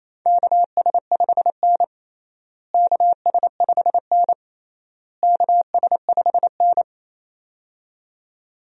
Morse code, heard as a keyed tone, about 28 wpm, 700 Hz. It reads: KS5N KS5N KS5N
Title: USA Call Signs - 4 Characters - Repeated 3x - 28wpm